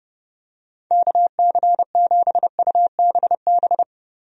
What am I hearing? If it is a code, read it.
Morse code, heard as a keyed tone, about 30 wpm, 700 Hz. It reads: KC7UBB
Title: USA Call Signs - 6 Characters - Repeated 2x - 30wpm